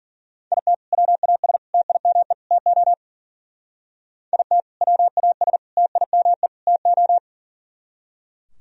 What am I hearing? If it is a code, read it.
Morse code, heard as a keyed tone, about 49 wpm, 700 Hz. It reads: IT WAS TIME TO IT WAS TIME TO
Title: Speed-Racing - Sets of 4 Words 40wpm